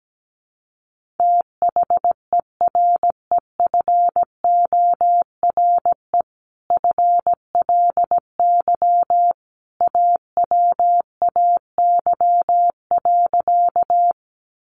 Morse code, heard as a keyed tone, about 17 wpm, 700 Hz. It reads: THEREFORE FLY AWAY.
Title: Sentences from Top 800 Words 17wpm